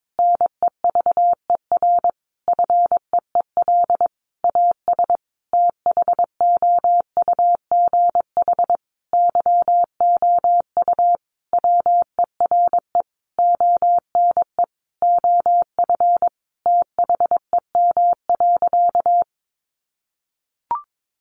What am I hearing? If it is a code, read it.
Morse code, heard as a keyed tone, about 22 wpm, 700 Hz. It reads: NEVER FEEL AS THOUGH YOU WERE ONE OF THEM.
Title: Sentences from Top 1500 Words 22wpm